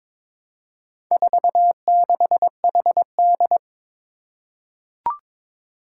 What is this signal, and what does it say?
Morse code, heard as a keyed tone, about 22 wpm, 700 Hz. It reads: 46HD
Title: Four Challenging Letter-Number Combinations 22wpm